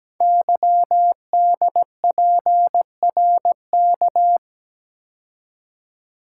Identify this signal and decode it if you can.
Morse code, heard as a keyed tone, about 17 wpm, 700 Hz. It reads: YDPRK
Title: Code Groups - 5 Letters - 17wpm